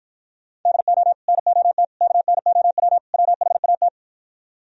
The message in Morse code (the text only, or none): DO NOT KNOW WHAT